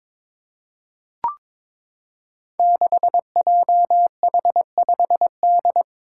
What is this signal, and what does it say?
Morse code, heard as a keyed tone, about 23 wpm, 700 Hz. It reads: 6JH5D
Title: Five Challenging Letter-Number Combinations 22wpm